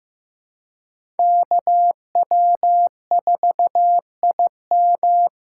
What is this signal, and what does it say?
Morse code, heard as a keyed tone, about 15 wpm, 700 Hz. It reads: KW4IM
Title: USA Call Signs - 5 Characters - 4x Character Spacing - 15wpm